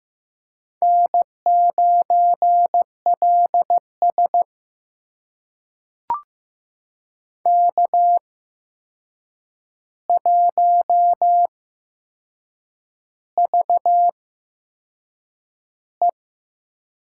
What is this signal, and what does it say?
Morse code, heard as a keyed tone, about 15 wpm, spaced as for 6 wpm, 700 Hz. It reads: N9LS K1VE
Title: USA Call Signs - 4 Characters - 8x Character Spacing - 15wpm